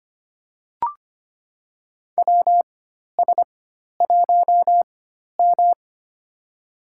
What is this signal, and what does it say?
Morse code, heard as a keyed tone, about 25 wpm, spaced as for 12 wpm, 700 Hz. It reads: WS1M
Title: USA Call Signs - 4 Characters - 4x Character Spacing - 25wpm